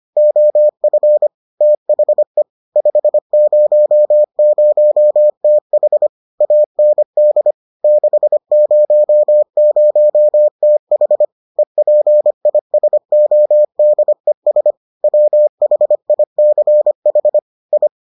Morse code, heard as a keyed tone, about 25 wpm, 600 Hz. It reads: OF THE 500TH AND 600TH EPISODES WHICH I